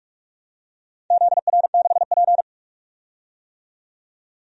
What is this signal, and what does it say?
Morse code, heard as a keyed tone, about 41 wpm, 700 Hz. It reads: ZR6P
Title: DXCC-100 Call Signs - 4 Characters - 45wpm